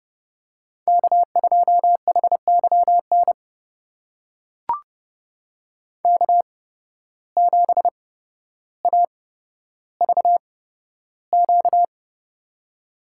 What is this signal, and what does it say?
Morse code, heard as a keyed tone, about 32 wpm, spaced as for 13 wpm, 700 Hz. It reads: K2HYN K7AVQ
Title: USA Call Signs - 5 Characters - 8x Character Spacing - 30wpm